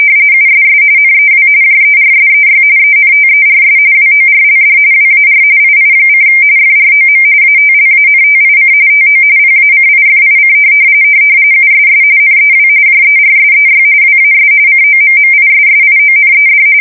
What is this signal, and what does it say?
RTTY, about 45.45 baud. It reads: USS NEW JERSEY ENTERS WORLD WAR II: FROM ATLANTIC TO PACIFIC USS NEW JERSEY WAS COMMISSIONED
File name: fsk170 / USS-New-Jersey